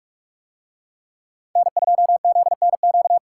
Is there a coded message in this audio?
Yes: Morse code from 1.6 to 3.2 s, heard as a keyed tone, about 43 wpm, 700 Hz. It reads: N2ZNQ